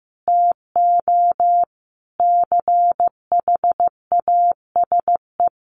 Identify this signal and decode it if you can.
Morse code, heard as a keyed tone, about 15 wpm, 700 Hz. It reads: TO CHASE
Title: Additional Easy Phrases 15wpm